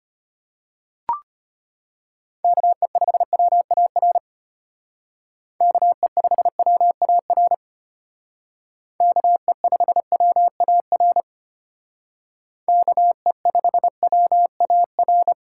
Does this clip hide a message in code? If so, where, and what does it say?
Yes: Morse code from 2.4 to 15.3 s, heard as a keyed tone, about 32 wpm, 700 Hz. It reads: KE5WAR KE5WAR KE5WAR KE5WAR